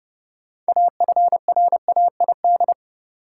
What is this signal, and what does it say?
Morse code, heard as a keyed tone, about 30 wpm, 700 Hz. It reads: AFRAID
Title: Sentences from Top 1500 Words - New Word Drill - 30wpm